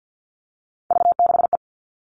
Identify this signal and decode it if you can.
Morse code, heard as a keyed tone, about 48 wpm, 700 Hz. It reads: V6E
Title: All Cut Numbers - 3-Digit 50wpm